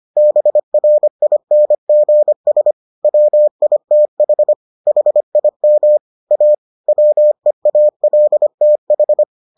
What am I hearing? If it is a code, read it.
Morse code, heard as a keyed tone, about 25 wpm, 600 Hz. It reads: BRINGS WITH HIM A WEALTH